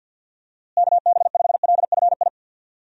Morse code, heard as a keyed tone, about 48 wpm, 700 Hz. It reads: KB5LFI
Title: USA Call Signs - 6 Characters - Repeated 3x - 50wpm